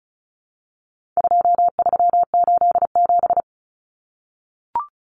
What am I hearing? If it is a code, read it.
Morse code, heard as a keyed tone, about 34 wpm, 700 Hz. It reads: 2387
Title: Common Cut Numbers - 4-Digit 35wpm